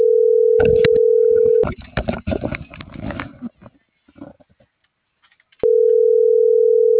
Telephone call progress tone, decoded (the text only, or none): ringback tone